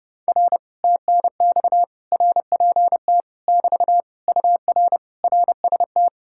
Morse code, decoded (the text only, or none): R TNX RPT <BT> UR RST